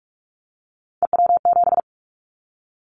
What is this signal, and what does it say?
Morse code, heard as a keyed tone, about 44 wpm, 700 Hz. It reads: EW7
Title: Callsign Prefixes for Top 100 DXCC - 45wpm